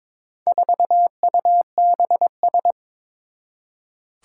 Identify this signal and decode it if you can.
Morse code, heard as a keyed tone, about 22 wpm, 700 Hz. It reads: 4UBS
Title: Four Challenging Letter-Number Combinations 22wpm